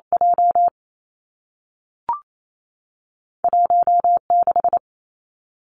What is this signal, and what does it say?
Morse code, heard as a keyed tone, about 28 wpm, 700 Hz. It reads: J 1 6
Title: Two Challenging Letter-Number Combinations 28wpm